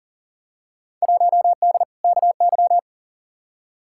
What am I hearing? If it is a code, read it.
Morse code, heard as a keyed tone, about 40 wpm, 700 Hz. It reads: 1D KY
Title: Field Day Exchanges - 40wpm